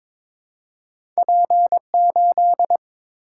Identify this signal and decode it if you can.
Morse code, heard as a keyed tone, about 22 wpm, 700 Hz. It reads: P8
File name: practice / 22wpm / Two Letter-Number Combinations - 22wpm